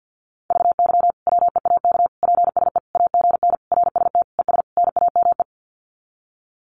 Morse code, heard as a keyed tone, about 50 wpm, 700 Hz. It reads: VY WEAK PSE AGN RST ES NAME
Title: Common QSO Phrases 50wpm